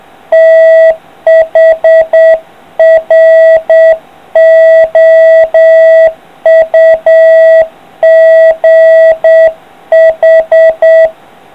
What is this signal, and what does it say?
Morse code, heard as a keyed tone, about 8 wpm, 650 Hz. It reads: THROUGH